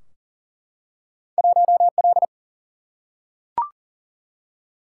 Morse code, heard as a keyed tone, about 42 wpm, 700 Hz. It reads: 1L
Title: Single-Letter Single-Number 40wpm